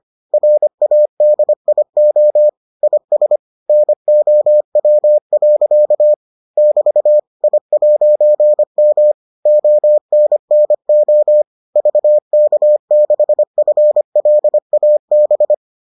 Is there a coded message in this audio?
Yes: Morse code from 0.3 to 15.5 s, heard as a keyed tone, about 25 wpm, 600 Hz. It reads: RADIO IS NOW. <BT> I'M ONNO VK6FLAB